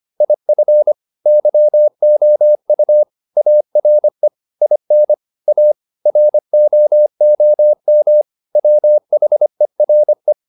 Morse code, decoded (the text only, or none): IF YOU ARE IN A ROOM WHERE